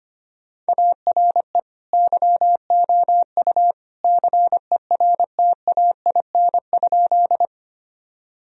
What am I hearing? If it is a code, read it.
Morse code, heard as a keyed tone, about 25 wpm, 700 Hz. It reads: ARE YOU CERTAIN?
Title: Sentences from Top 1000 Words - Review Part 2 - 25wpm